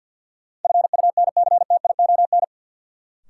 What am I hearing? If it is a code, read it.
Morse code, heard as a keyed tone, about 52 wpm, 700 Hz. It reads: FUNCTION
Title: Sentences from Top 1100 Words - New Word Drill Alt - 50wpm